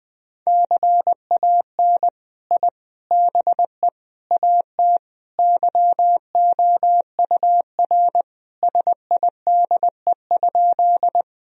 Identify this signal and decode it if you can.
Morse code, heard as a keyed tone, about 20 wpm, 700 Hz. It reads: CAN I BE AT YOUR SIDE?
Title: Sentences from Top 1000 Words - Review Part 1 - 20wpm